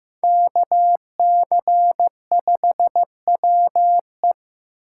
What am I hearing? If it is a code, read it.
Morse code, heard as a keyed tone, about 18 wpm, 700 Hz. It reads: KC5WE
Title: Speed Racing - USA 2x2 Call Signs 15wpm